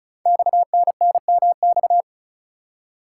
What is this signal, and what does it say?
Morse code, heard as a keyed tone, about 36 wpm, 690 Hz. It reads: XNNMX
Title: Code Groups - 5 Letters - 35wpm